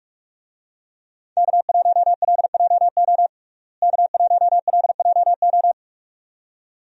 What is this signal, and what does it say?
Morse code, heard as a keyed tone, about 44 wpm, 700 Hz. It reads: K1LJO K1LJO
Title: USA Call Signs - 5 Characters - Repeated 2x - 45wpm